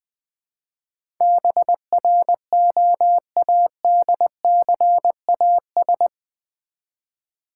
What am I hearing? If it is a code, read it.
Morse code, heard as a keyed tone, about 20 wpm, 700 Hz. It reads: BROADCAS
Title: Build A Word - 4 Letters At A Time - 11-Letter Words - 20wpm